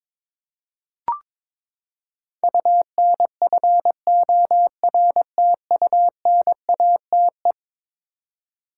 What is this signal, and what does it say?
Morse code, heard as a keyed tone, about 22 wpm, 700 Hz. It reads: UNFORTUNATE
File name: practice / 22wpm / Eleven Letter Words - 22wpm